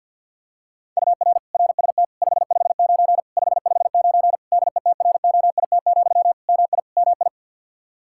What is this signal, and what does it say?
Morse code, heard as a keyed tone, about 51 wpm, 700 Hz. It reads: UR RST 559 559 DETROIT, MI MI
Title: Practice QSO - Exchange 1 - Beginning - 50wpm